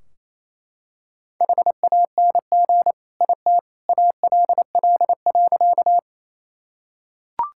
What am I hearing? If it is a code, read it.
Morse code, heard as a keyed tone, about 27 wpm, 700 Hz. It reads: HANG IT ALL.
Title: Sentences from Top 1600 Words 28wpm